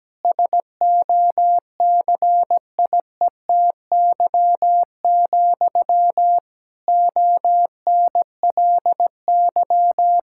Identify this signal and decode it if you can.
Morse code, heard as a keyed tone, about 17 wpm, 700 Hz. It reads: SOCIETY, ONLY
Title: Sentences from Top 1000 Words - Review Part 6 - 17wpm